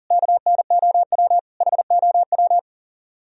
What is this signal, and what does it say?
Morse code, heard as a keyed tone, about 52 wpm, 700 Hz. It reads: KNOW HOW